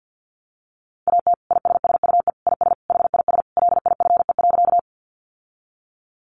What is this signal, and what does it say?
Morse code, heard as a keyed tone, about 50 wpm, 700 Hz. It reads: AT ISSUE IS HIS LIFE.
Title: Sentences from Top 900 Words 50wpm